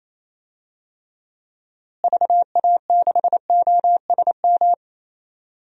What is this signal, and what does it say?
Morse code, heard as a keyed tone, about 28 wpm, 700 Hz. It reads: VA6OSM
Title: DXCC-100 Call Signs - 6 Characters - Repeated 2x - 28wpm